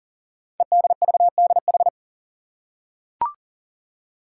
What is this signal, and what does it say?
Morse code, heard as a keyed tone, about 38 wpm, 690 Hz. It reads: EDVDH
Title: Code Groups - 5 Letters - 40wpm